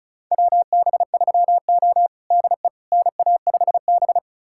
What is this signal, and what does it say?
Morse code, heard as a keyed tone, about 36 wpm, 700 Hz. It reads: WB3O DE NA5B
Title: USA Callsign DE Callsign - 4 Characters - 35wpm